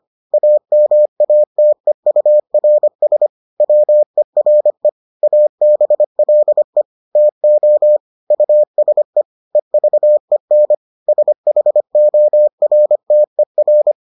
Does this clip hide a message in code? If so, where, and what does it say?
Yes: Morse code from 0.3 to 13.9 s, heard as a keyed tone, about 25 wpm, 600 Hz. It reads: AMATEURS WERE ABLE TO USE EVEN SHORTER